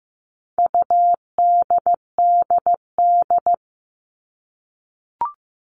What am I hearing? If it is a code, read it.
Morse code, heard as a keyed tone, about 15 wpm, 700 Hz. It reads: UDDD
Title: All Cut Numbers - 4-Digit 15wpm